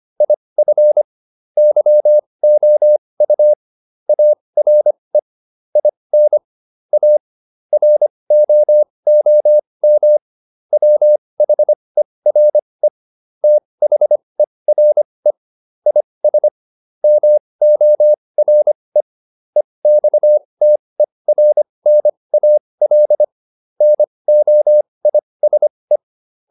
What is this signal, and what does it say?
Morse code, heard as a keyed tone, about 25 wpm, spaced as for 20 wpm, 600 Hz. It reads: IF YOU ARE IN A ROOM WHERE THERE IS MORE EXTERNAL NOISE